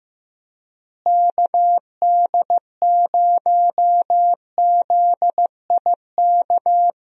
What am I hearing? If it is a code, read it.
Morse code, heard as a keyed tone, about 15 wpm, 700 Hz. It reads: KD0ZIK
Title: USA Call Signs - 6 Characters - Repeated 3x - 15wpm